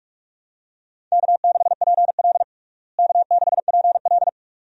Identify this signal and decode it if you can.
Morse code, heard as a keyed tone, about 46 wpm, 700 Hz. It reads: K6PL K6PL
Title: USA Call Signs - 4 Characters - Repeated 2x - 45wpm